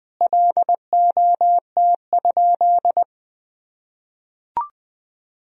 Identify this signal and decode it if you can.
Morse code, heard as a keyed tone, about 20 wpm, 700 Hz. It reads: LOT?
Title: Sentences from Top 1000 Words - Review Part 4 - 20wpm